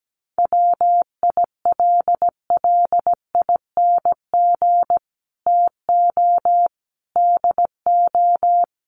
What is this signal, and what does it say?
Morse code, heard as a keyed tone, about 17 wpm, 700 Hz. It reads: WILLING TO DO
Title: Sentences from Top 1400 Words 17wpm